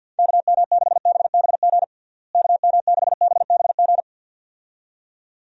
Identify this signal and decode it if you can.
Morse code, heard as a keyed tone, about 48 wpm, 700 Hz. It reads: KM6BBG KM6BBG
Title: USA Call Signs - 6 Characters - Repeated 2x - 50wpm